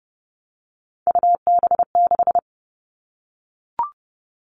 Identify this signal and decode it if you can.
Morse code, heard as a keyed tone, about 30 wpm, 700 Hz. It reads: UB6
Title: All Cut Numbers - 3-Digit 30wpm